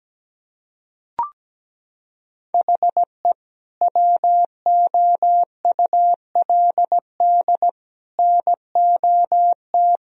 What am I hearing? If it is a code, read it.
Morse code, heard as a keyed tone, about 17 wpm, 700 Hz. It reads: HE WOULD NOT
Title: Sentences from Top 1000 Words - Review Part 2 - 17wpm